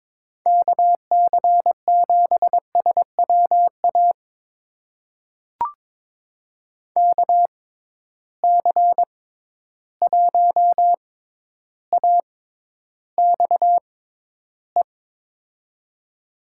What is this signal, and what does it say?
Morse code, heard as a keyed tone, about 22 wpm, spaced as for 11 wpm, 700 Hz. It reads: KC7SWA KC1AXE
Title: USA Call Signs - 6 Characters - 6x Character Spacing - 22wpm